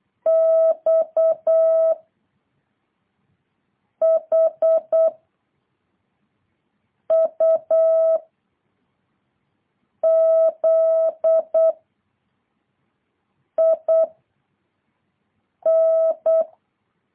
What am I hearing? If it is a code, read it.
Morse code, heard as a keyed tone, about 8 wpm, 650 Hz. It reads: X H U Z I N